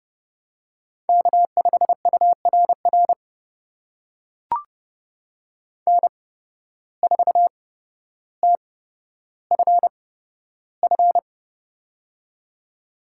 Morse code, heard as a keyed tone, about 30 wpm, spaced as for 13 wpm, 700 Hz. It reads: K5URR N4TFF
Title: USA Call Signs - 5 Characters - 8x Character Spacing - 30wpm